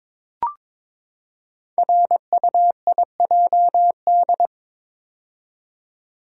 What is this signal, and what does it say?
Morse code, heard as a keyed tone, about 22 wpm, 700 Hz. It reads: RUIJD
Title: Code Groups - 5 Letters - 22wpm